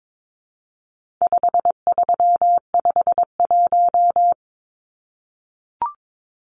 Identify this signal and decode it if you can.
Morse code, heard as a keyed tone, about 22 wpm, 700 Hz. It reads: 5351
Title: Numbers - 4-Digit 22wpm